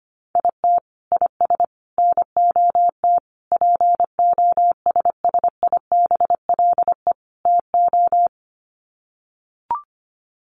Morse code, heard as a keyed tone, about 25 wpm, 700 Hz. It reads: IT IS NOT POSSIBLE TO
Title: Sets of 5 Words Encore2 25wpm